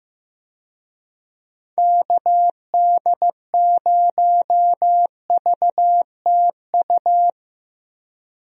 Morse code, heard as a keyed tone, about 15 wpm, 700 Hz. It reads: KD0VTU
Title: USA Call Signs - 6 Characters - Repeated 3x - 15wpm